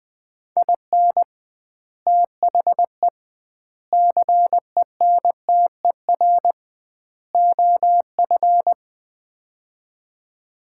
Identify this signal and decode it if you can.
Morse code, heard as a keyed tone, about 20 wpm, 700 Hz. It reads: IN THE CENTER OF
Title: 2x Word-Spacing - Sets of 4 Words - 20wpm